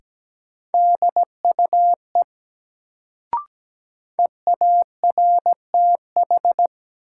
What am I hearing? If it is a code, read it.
Morse code, heard as a keyed tone, about 17 wpm, 700 Hz. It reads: DUE EARTH
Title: Sets of 5 Words Encore - New Word Drill - 17wpm